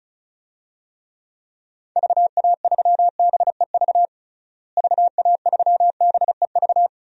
Morse code, heard as a keyed tone, about 34 wpm, 700 Hz. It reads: VA3BEV VA3BEV
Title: DXCC-100 Call Signs - 6 Characters - Repeated 3x - 35wpm